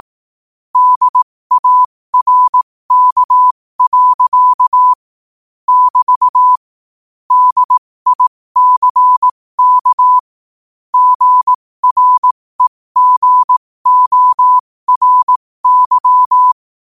Morse code, heard as a keyed tone, about 18 wpm, spaced as for 15 wpm, 1000 Hz. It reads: DARK. <BT> DICK GREGORY
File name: qotdfiles / 15wpm